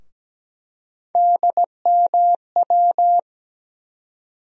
Morse code, heard as a keyed tone, about 17 wpm, 700 Hz. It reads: DMW
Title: USA Call Sign Postfixes - 3 Letters 17wpm